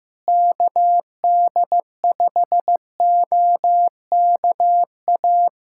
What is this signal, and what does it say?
Morse code, heard as a keyed tone, about 15 wpm, 700 Hz. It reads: KD5OKA
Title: USA Callsign DE Callsign - 6 Characters - 15wpm